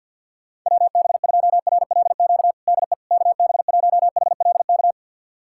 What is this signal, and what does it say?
Morse code, heard as a keyed tone, about 48 wpm, 700 Hz. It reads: WB2RLQ DE KB1HLK
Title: USA Callsign DE Callsign - 6 Characters - 50wpm